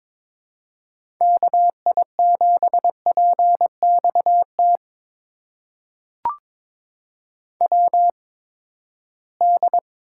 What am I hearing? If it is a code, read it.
Morse code, heard as a keyed tone, about 22 wpm, spaced as for 15 wpm, 700 Hz. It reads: KI7PXT WD
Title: USA Call Signs - 6 Characters - 8x Character Spacing - 22wpm